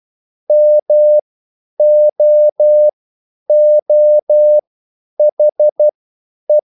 Morse code, heard as a keyed tone, about 12 wpm, spaced as for 9 wpm, 600 Hz. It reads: MOOHE
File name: QSL / CW lesson